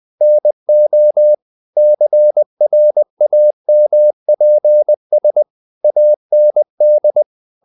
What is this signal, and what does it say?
Morse code, heard as a keyed tone, about 20 wpm, 600 Hz. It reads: NO CRAMPS AND
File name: the old man and the sea / the old man and the sea part 1 20wpm